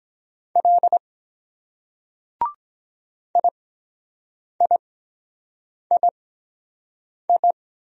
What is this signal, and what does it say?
Morse code, heard as a keyed tone, about 23 wpm, spaced as for 7 wpm, 700 Hz. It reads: L IIII